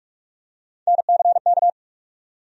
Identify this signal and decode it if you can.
Morse code, heard as a keyed tone, about 45 wpm, 700 Hz. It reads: NCK
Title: USA Call Sign Postfixes - 3 Letters 45wpm